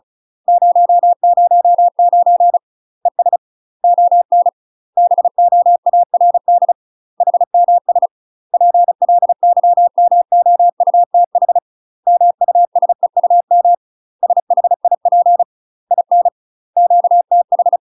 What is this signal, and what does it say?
Morse code, heard as a keyed tone, about 36 wpm, 700 Hz. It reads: 009 ES ON BOARD HMS PLYMOUTH MUSEUM SHIP IN QTH
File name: UK MASTER QSOs TRAINING / 35WPM / UK MASTER QSOs 0018